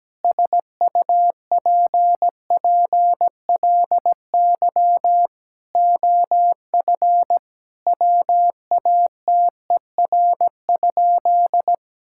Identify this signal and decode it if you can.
Morse code, heard as a keyed tone, about 17 wpm, 700 Hz. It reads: SUPPLY OF WATER?
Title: Sentences from Top 1000 Words - Review Part 5 - 17wpm